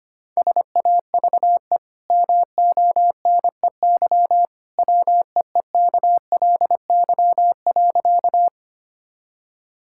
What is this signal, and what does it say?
Morse code, heard as a keyed tone, about 25 wpm, 700 Hz. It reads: SAVE MONEY WEEKLY.
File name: Easy Sentences with Hints - 25wpm